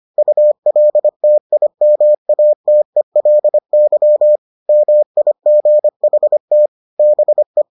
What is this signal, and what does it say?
Morse code, heard as a keyed tone, about 25 wpm, 600 Hz. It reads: ULTIMATELY MIGHT BE